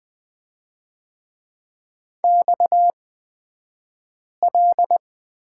Morse code, heard as a keyed tone, about 20 wpm, spaced as for 5 wpm, 700 Hz. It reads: XL
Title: 242 Lesson 39d - Letter z - Two Characters 20wpm